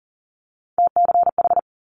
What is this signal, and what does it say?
Morse code, heard as a keyed tone, about 40 wpm, 700 Hz. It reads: TCH
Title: Multi-letter Phonemes Quick Fire 40wpm